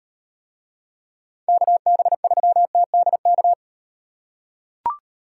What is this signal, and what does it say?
Morse code, heard as a keyed tone, about 38 wpm, 700 Hz. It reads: KB3TDK